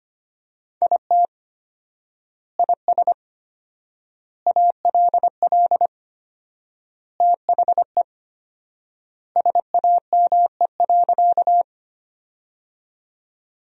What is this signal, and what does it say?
Morse code, heard as a keyed tone, about 25 wpm, 700 Hz. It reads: IT IS ALL THE SAME.
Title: 4x Word-Spacing - Sentences from Top 200 Words - 25wpm